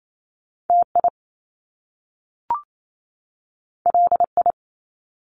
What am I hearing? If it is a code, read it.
Morse code, heard as a keyed tone, about 28 wpm, 700 Hz. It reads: TI LI
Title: Top Two Letter Combinations in Words 28wpm